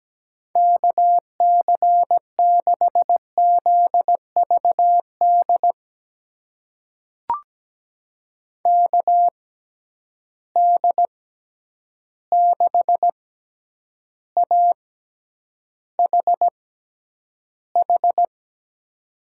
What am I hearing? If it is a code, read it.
Morse code, heard as a keyed tone, about 17 wpm, spaced as for 9 wpm, 700 Hz. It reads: KC6ZVD KD6AHH
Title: USA Call Signs - 6 Characters - 6x Character Spacing - 17wpm